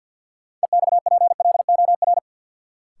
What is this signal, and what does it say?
Morse code, heard as a keyed tone, about 53 wpm, 700 Hz. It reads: EXPLOR
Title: Sentences from Top 1100 Words - New Word Drill Alt - 50wpm